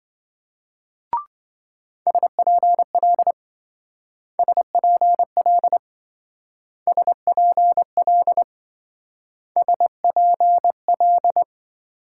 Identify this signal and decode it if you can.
Morse code, heard as a keyed tone, about 25 wpm, 700 Hz. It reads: SPL SPL SPL SPL
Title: Speed-Racing Multi-letter Phonemes - 20wpm